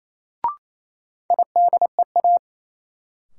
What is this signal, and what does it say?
Morse code, heard as a keyed tone, about 28 wpm, 700 Hz. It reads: IDEA
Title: Sentences from Top 300 Words - New Word Drill - 28wpm